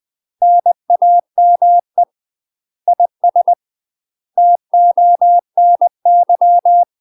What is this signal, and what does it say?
Morse code, heard as a keyed tone, about 20 wpm, 700 Hz. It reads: NAME IS TONY
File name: UK MASTER QSOs TRAINING / 20WPM / UK MASTER QSOs 0045